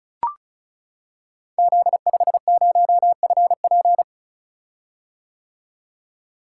Morse code, heard as a keyed tone, about 36 wpm, 700 Hz. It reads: Z50FP
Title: Code Groups - 5 Letter:Numbers - 35wpm